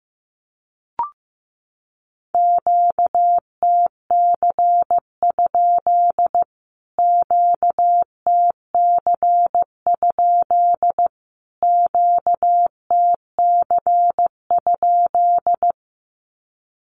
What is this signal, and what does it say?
Morse code, heard as a keyed tone, about 15 wpm, 700 Hz. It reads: QTC? QTC? QTC?
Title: Q-Codes Repeated 3x 15wpm